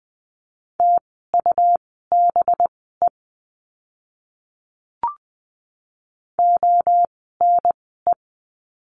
Morse code, heard as a keyed tone, about 20 wpm, spaced as for 14 wpm, 700 Hz. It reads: TUBE ONE
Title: 185 Lesson 30f - Letter b - Words 20wpm